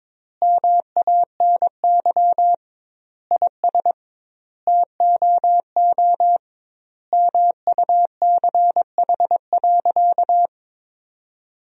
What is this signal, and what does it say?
Morse code, heard as a keyed tone, about 22 wpm, 700 Hz. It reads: MANY IS TOO MUCH.
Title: 2x Word-Spacing - Sentences from Top 400 Words - 22wpm